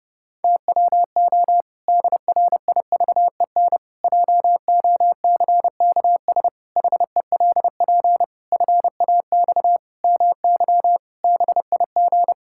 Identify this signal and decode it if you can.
Morse code, heard as a keyed tone, about 32 wpm, 700 Hz. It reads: TWO DRIVEN JOCKS HELP FAX MY BIG
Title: Pangram Challenge v2 - 30wpm